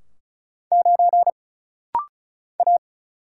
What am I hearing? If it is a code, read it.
Morse code, heard as a keyed tone, about 35 wpm, 700 Hz. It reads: 9 A
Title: Single Letter-Number - Rapid-Fire 35wpm